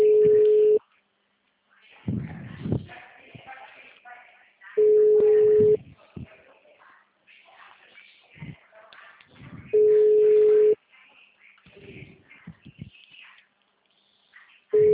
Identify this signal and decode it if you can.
Ringback tone (Europe, 425 Hz), with speech.
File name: out- -2012- - - .16888